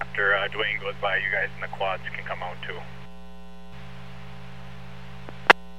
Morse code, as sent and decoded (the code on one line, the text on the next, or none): none